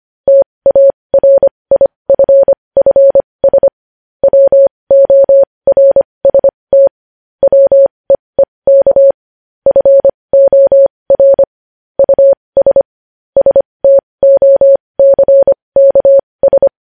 Morse code, heard as a keyed tone, about 25 wpm, spaced as for 20 wpm, 550 Hz. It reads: TARIFFS WORST WEEK FOR US STOCKS
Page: News headlines in morse code at 20 wpm